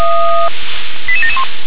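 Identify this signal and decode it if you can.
Five-tone selective call (ZVEI), heard as quick runs of five tones, tones of about 70 ms. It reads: 88771
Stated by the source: ZVEI five-tone call 88954 received by SDR.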